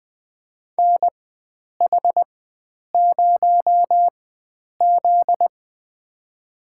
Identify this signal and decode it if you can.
Morse code, heard as a keyed tone, about 21 wpm, spaced as for 9 wpm, 700 Hz. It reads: NH0Z
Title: USA Call Signs - 4x Character Spacing - 20wpm